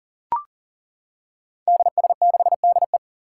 Morse code, heard as a keyed tone, about 40 wpm, 700 Hz. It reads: DS6DE
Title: Five Challenging Letter-Number Combinations 40wpm